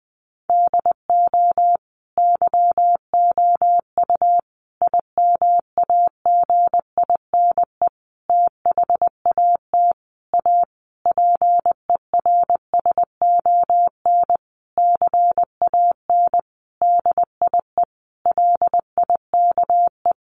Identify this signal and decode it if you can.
Morse code, heard as a keyed tone, about 20 wpm, 700 Hz. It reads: DO YOU IMAGINE THAT A PERSON CAN DIE LIKE
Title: Sentences from Top 1400 Words 20wpm